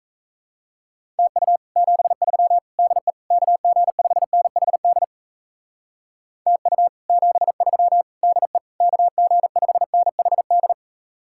Morse code, heard as a keyed tone, about 35 wpm, 700 Hz. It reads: TU 73 DE KG5NHD TU 73 DE KG5NHD
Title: Speed-Racing Common QSO Phrases - 28wpm